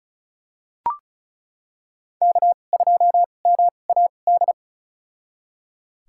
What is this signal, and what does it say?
Morse code, heard as a keyed tone, about 36 wpm, spaced as for 25 wpm, 700 Hz. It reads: K2MAD
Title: USA Call Signs - 2x Character Spacing - 35wpm